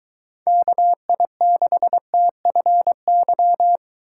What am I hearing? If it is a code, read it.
Morse code, heard as a keyed tone, about 23 wpm, 700 Hz. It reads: KI6TFY